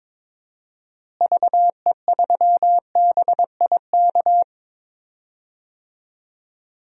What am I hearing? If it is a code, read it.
Morse code, heard as a keyed tone, about 22 wpm, 700 Hz. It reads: VE3BIK
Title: DXCC-100 Call Signs - 6 Characters - Repeated 3x - 22wpm